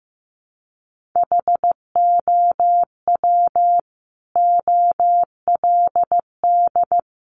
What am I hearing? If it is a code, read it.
Morse code, heard as a keyed tone, about 15 wpm, 700 Hz. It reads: HOW OLD
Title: Sentences from Top 200 Words 15wpm